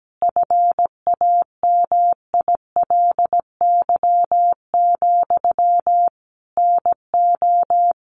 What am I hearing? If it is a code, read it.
Morse code, heard as a keyed tone, about 17 wpm, 700 Hz. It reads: FAMILY, NO
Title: Sentences from Top 1200 Words 17wpm